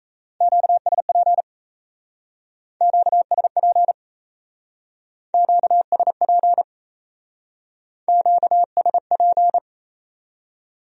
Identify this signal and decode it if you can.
Morse code, heard as a keyed tone, about 35 wpm, 700 Hz. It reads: QSP QSP QSP QSP
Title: Speed-Racing Q-Codes - 28wpm